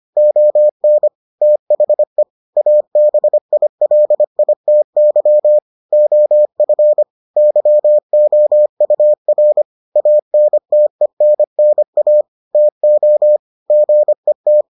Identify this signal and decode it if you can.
Morse code, heard as a keyed tone, about 25 wpm, 600 Hz. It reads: ON THE ABILITY OF YOUR ANTENNA TO GET